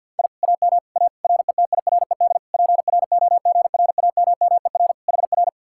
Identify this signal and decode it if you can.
Morse code, heard as a keyed tone, about 62 wpm, 700 Hz. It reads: I AM A RETIRED PROGRAMMER HR
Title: Speed-Racing Common QSO Phrases - 50wpm